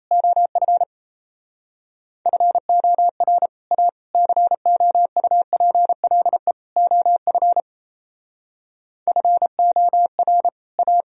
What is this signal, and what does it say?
Morse code, heard as a keyed tone, about 35 wpm, 700 Hz. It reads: OF FOR A COUPLE OF FOR A